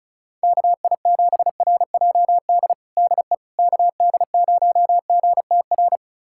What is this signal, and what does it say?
Morse code, heard as a keyed tone, about 36 wpm, 700 Hz. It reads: KI7RJD DE KD0GTR
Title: USA Callsign DE Callsign - 6 Characters - 35wpm